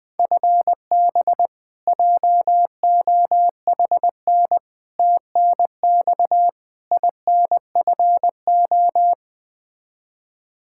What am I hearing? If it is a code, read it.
Morse code, heard as a keyed tone, about 20 wpm, 700 Hz. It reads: FB JOHN TNX INFO
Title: Practice QSO - Exchange 3 - More Info - 20wpm